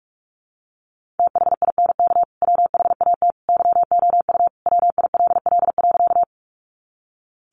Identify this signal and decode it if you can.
Morse code, heard as a keyed tone, about 44 wpm, 700 Hz. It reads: THINK WHAT YOU WILL.
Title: Sentences from Top 200 Words 45wpm